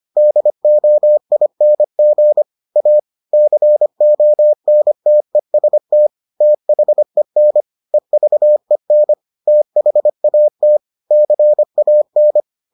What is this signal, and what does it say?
Morse code, heard as a keyed tone, about 25 wpm, 600 Hz. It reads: DOING A CONTEST THEN EVEN THAT CAN